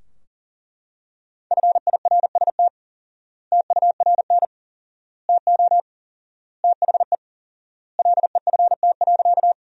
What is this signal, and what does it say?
Morse code, heard as a keyed tone, about 40 wpm, 700 Hz. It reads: FIRST TURN TO THE LEFT.
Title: 4x Word-Spacing - Sentences from Top 300 Words - 40wpm